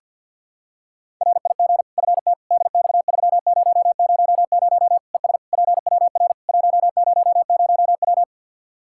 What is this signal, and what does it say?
Morse code, heard as a keyed tone, about 48 wpm, 700 Hz. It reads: RIG FT DX3000 ES PWR 100W
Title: Practice QSO - Exchange 2 - Info - 50wpm